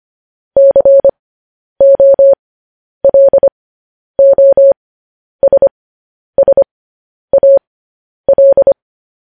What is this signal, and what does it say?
Morse code, heard as a keyed tone, about 25 wpm, spaced as for 10 wpm, 550 Hz. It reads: COLOSSAL
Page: News Headlines in Morse Code at 10 WPM